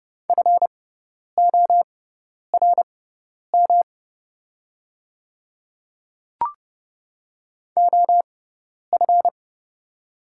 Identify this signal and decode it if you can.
Morse code, heard as a keyed tone, about 31 wpm, spaced as for 10 wpm, 700 Hz. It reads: FORM OF
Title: 6x Character Spacing - Top 300 Words - 30wpm